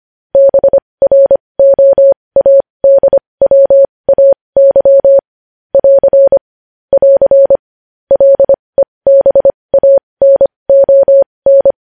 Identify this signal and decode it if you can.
Morse code, heard as a keyed tone, about 25 wpm, spaced as for 20 wpm, 550 Hz. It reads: BROADWAY <AR> <AR> LEBANON
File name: podcast / News headlines-1x-25x20